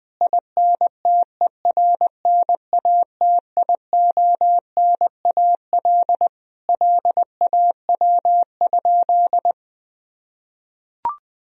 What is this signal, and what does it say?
Morse code, heard as a keyed tone, about 20 wpm, 700 Hz. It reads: INTERNATIONAL LAW?
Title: Sentences from Top 1000 Words - Review Part 4 - 20wpm